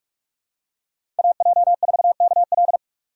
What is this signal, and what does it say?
Morse code, heard as a keyed tone, about 44 wpm, 690 Hz. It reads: AJ4KL
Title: USA Call Signs - 5 Characters - 2x Character Spacing - 45wpm